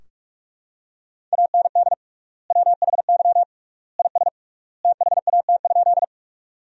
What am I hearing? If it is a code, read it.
Morse code, heard as a keyed tone, about 46 wpm, 700 Hz. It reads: AND WHY IS THAT?
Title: 3x Word-Spacing - Sentences from Top 200 Words - 45wpm